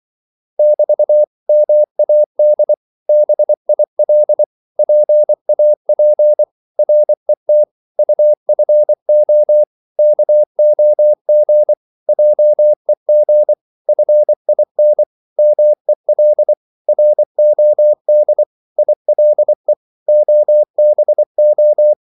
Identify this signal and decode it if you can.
Morse code, heard as a keyed tone, about 24 wpm, 600 Hz. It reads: <BT> MAD BIL PAP RET UFO KOG JEG FIN MEL ROD ILE OBO
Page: Korte danske ord 24wpm